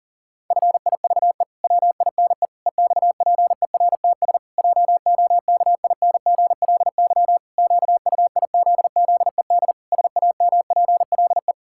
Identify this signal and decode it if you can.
Morse code, heard as a keyed tone, about 40 wpm, 700 Hz. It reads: FIVE WINE EXPERTS JOKINGLY QUIZZED SAMPLE
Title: Pangram Challenge v2 - 40wpm